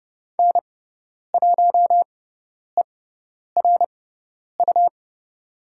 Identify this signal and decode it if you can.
Morse code, heard as a keyed tone, about 31 wpm, spaced as for 10 wpm, 690 Hz. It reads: N1ERU
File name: Lesson 18g - Letter m - Call Signs 30wpm